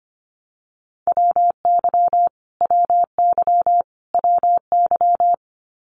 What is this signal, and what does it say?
Morse code, heard as a keyed tone, about 25 wpm, 700 Hz. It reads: WY WY WY
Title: US States Repeated 3x 25wpm